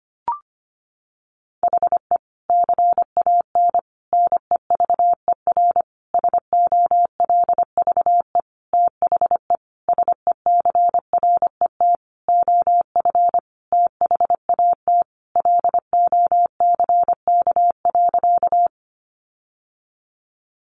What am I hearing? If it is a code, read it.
Morse code, heard as a keyed tone, about 25 wpm, 700 Hz. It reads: HE CAN NEVER SOLVE THE SECRET OF THAT LOCK.
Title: Sentences from Top 1300 Words 25wpm